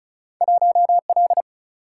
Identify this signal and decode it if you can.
Morse code, heard as a keyed tone, about 34 wpm, 700 Hz. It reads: 1L
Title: Single-Letter Single-Number 35wpm